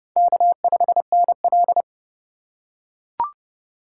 Morse code, heard as a keyed tone, about 27 wpm, 700 Hz. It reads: K5NL